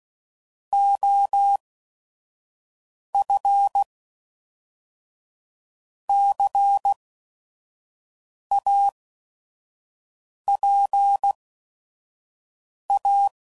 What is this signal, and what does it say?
Morse code, heard as a keyed tone, about 16 wpm, spaced as for 5 wpm, 800 Hz. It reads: OF CAPA
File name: CW Code Groups @5WPM / Technical Subjects / XC